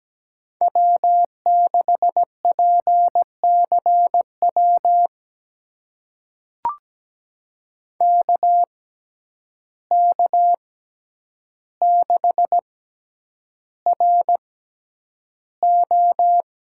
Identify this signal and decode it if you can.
Morse code, heard as a keyed tone, about 17 wpm, spaced as for 9 wpm, 700 Hz. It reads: W6PCW KK6RO
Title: USA Call Signs - 5 Characters - 6x Character Spacing - 17wpm